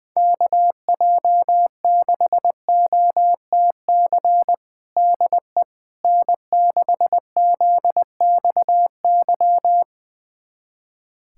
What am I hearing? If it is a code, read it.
Morse code, heard as a keyed tone, about 20 wpm, 700 Hz. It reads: KJ6OTC DE N6ZXY
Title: USA Callsign DE Callsign 20wpm